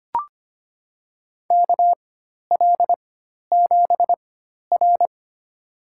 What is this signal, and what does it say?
Morse code, heard as a keyed tone, about 25 wpm, spaced as for 12 wpm, 700 Hz. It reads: KL7R
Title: USA Call Signs - 4x Character Spacing - 25wpm